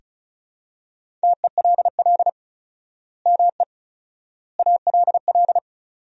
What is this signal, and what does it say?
Morse code, heard as a keyed tone, about 34 wpm, 700 Hz. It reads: TELL ME ALL
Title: 4x Word-Spacing - Sentences from Top 200 Words - 35wpm